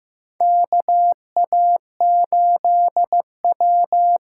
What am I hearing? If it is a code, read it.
Morse code, heard as a keyed tone, about 15 wpm, 700 Hz. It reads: KA8W
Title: USA Call Signs - 6 Characters - 8x Character Spacing - 15wpm